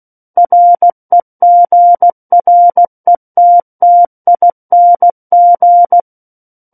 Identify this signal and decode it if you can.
Morse code, heard as a keyed tone, about 16 wpm, 700 Hz. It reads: REGRETTING
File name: Book One / 07-Chapter 7 - 16 WPM